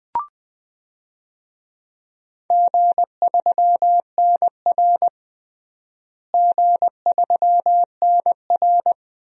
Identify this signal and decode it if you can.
Morse code, heard as a keyed tone, about 20 wpm, 700 Hz. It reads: G3NR G3NR
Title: DXCC-100 Call Signs - 4 Characters - Repeated 2x - 20wpm